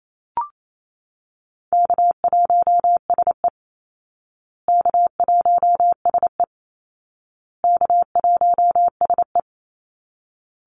Morse code, heard as a keyed tone, about 28 wpm, 700 Hz. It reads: K1SE K1SE K1SE
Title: Call Signs Repeated 2x - 28wpm